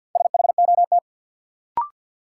Morse code, heard as a keyed tone, about 53 wpm, 700 Hz. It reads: SHOT
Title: Sentences from Top 1100 Words - New Word Drill - 50wpm